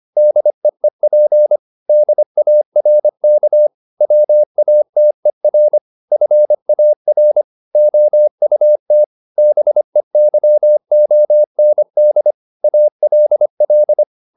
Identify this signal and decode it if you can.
Morse code, heard as a keyed tone, about 25 wpm, 600 Hz. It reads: DEEP DARK WATER FAR OUT BEYOND ALL
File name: the old man and the sea part 1 25wpm